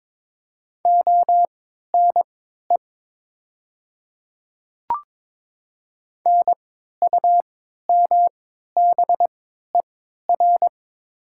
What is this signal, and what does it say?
Morse code, heard as a keyed tone, about 22 wpm, spaced as for 12 wpm, 700 Hz. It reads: ONE NUMBER
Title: 3x Character Spacing - Top 300 Words - 22wpm